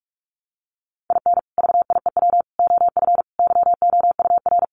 Morse code, heard as a keyed tone, about 44 wpm, 700 Hz. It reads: IN VIEW OF YOUR
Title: Sentences from Top 900 Words 45wpm